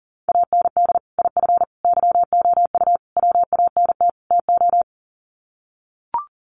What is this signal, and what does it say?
Morse code, heard as a keyed tone, about 40 wpm, 700 Hz. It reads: AND IF YOU WANT TO
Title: Sets of 5 Words Encore 40wpm